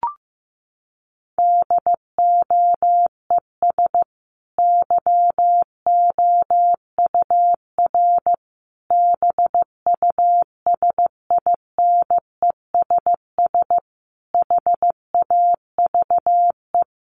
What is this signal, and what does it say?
Morse code, heard as a keyed tone, about 15 wpm, 700 Hz. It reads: DOES YOUR BUSINESS HAVE
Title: Sentences from Top 1200 Words 15wpm